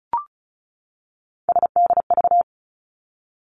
Morse code, heard as a keyed tone, about 36 wpm, 700 Hz. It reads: SDV
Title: Three Challenging Letter-Number Combinations 35wpm